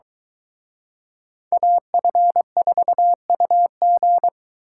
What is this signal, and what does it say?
Morse code, heard as a keyed tone, about 22 wpm, 700 Hz. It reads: AF4UG